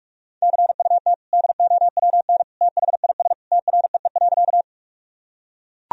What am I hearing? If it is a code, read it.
Morse code, heard as a keyed tone, about 41 wpm, 700 Hz. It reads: CUT DOWN THIS TREE.
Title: Sentences from Top 1000 Words - Review Part 1 - 45wpm